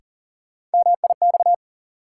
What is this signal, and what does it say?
Morse code, heard as a keyed tone, about 40 wpm, 700 Hz. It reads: MIX
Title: Sentences from Top 1100 Words - New Word Drill Alt - 40wpm